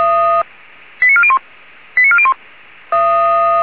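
Five-tone selective call (ZVEI), heard as quick runs of five tones, tones of about 70 ms. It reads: 89401 89401
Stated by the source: ZVEI five-tone selective call 89374 received by SDR.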